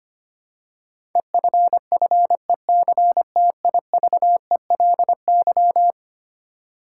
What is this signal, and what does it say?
Morse code, heard as a keyed tone, about 25 wpm, 700 Hz. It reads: EFFECTIVELY
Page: morse-code-ninja-practice-25wpm